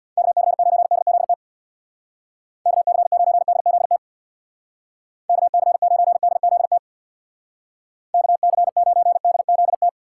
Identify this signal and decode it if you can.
Morse code, heard as a keyed tone, about 60 wpm, 700 Hz. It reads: KC9DZT KC9DZT KC9DZT KC9DZT
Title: Speed Racing - USA 2x3 Call Signs 50wpm